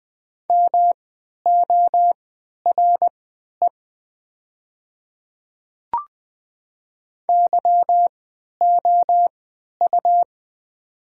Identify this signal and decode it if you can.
Morse code, heard as a keyed tone, about 20 wpm, spaced as for 11 wpm, 700 Hz. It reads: MORE YOU
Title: 3x Character Spacing - Top 100 Words - 20wpm